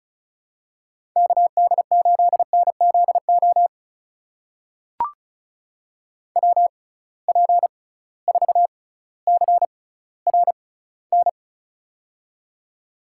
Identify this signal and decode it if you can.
Morse code, heard as a keyed tone, about 36 wpm, spaced as for 18 wpm, 700 Hz. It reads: KD8NZO WP4CRN
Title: USA Call Signs - 6 Characters - 6x Character Spacing - 35wpm